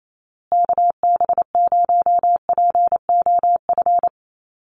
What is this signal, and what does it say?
Morse code, heard as a keyed tone, about 28 wpm, 700 Hz. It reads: KB0POF
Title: USA Call Signs Repeated 2x 28wpm